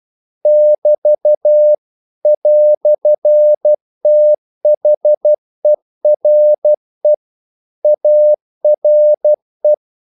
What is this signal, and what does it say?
Morse code, heard as a keyed tone, about 12 wpm, 600 Hz. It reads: <BT> "THERE ARE
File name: CW / Practice / EnglishQuotations0112